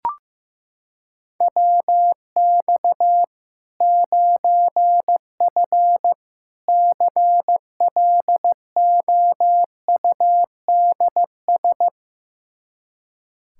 Morse code, heard as a keyed tone, about 15 wpm, 700 Hz. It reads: WX 9F CLOUDS
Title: K1USN SST QSO Practice 15wpm